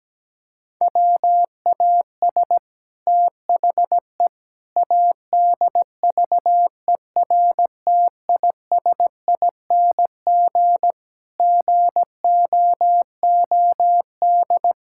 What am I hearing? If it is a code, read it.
Morse code, heard as a keyed tone, about 17 wpm, 700 Hz. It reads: WAS THE ADVERTISING GOOD
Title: Sentences from Top 1000 Words - Review Part 6 - 17wpm